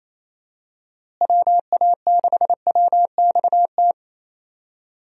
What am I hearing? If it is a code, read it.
Morse code, heard as a keyed tone, about 28 wpm, 700 Hz. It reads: WA6WXT
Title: USA Call Signs - 6 Characters - Repeated 2x - 28wpm